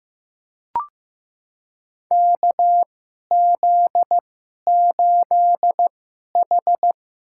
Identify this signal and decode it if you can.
Morse code, heard as a keyed tone, about 15 wpm, spaced as for 11 wpm, 700 Hz. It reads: KZ8H
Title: USA Call Signs - 2x Character Spacing - 15wpm